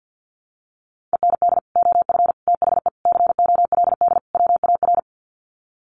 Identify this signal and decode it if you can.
Morse code, heard as a keyed tone, about 50 wpm, 700 Hz. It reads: END OF THE COLD WAR
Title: Sets of 5 Words Encore 50wpm